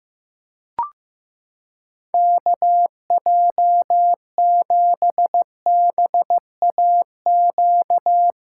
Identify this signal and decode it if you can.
Morse code, heard as a keyed tone, about 15 wpm, 700 Hz. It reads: KJ7BAQ
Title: USA Callsign DE Callsign 15wpm